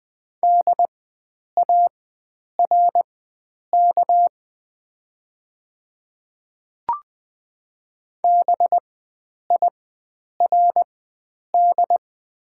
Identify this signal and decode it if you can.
Morse code, heard as a keyed tone, about 20 wpm, spaced as for 9 wpm, 700 Hz. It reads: DARK BIRD
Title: 4x Character Spacing - Top 500 Words - 20wpm